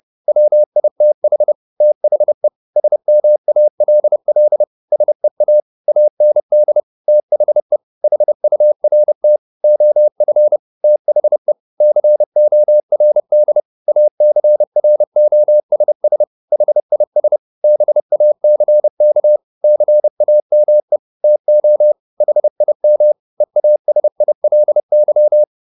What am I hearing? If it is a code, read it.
Morse code, heard as a keyed tone, about 30 wpm, 600 Hz. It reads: WITH THE SMALL SEA AND THE HURT OF THE CORD ACROSS HIS BACK CAME TO HIM EASILY
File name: the old man and the sea / the old man and the sea 1 30wpm